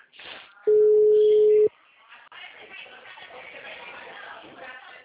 Ringback tone (Europe, 425 Hz).